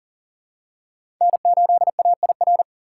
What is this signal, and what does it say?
Morse code, heard as a keyed tone, about 37 wpm, 700 Hz. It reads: N8AIR
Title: USA Call Signs - 5 Characters - 4x Character Spacing - 40wpm